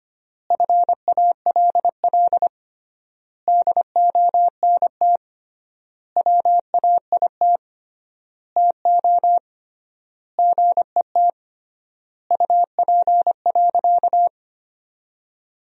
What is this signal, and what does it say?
Morse code, heard as a keyed tone, about 25 wpm, 700 Hz. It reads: FALL DONT WAIT TO GET UP.
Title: 3x Word-Spacing - Sentences from Top 500 Words - 25wpm